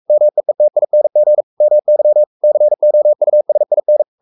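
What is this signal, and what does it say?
Morse code, heard as a keyed tone, about 44 wpm, 600 Hz. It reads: MEETING MY COUSIN
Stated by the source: Morse code at 43 wpm sending Ken Follett's novel Never in English, from CW med Gnister.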